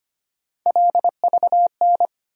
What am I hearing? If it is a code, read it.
Morse code, heard as a keyed tone, about 25 wpm, 700 Hz. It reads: LVN
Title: USA Call Sign Postfixes - 3 Letters 25wpm